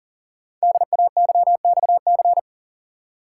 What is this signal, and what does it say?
Morse code, heard as a keyed tone, about 39 wpm, 700 Hz. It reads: DAYXC
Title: Code Groups - 5 Letters - 40wpm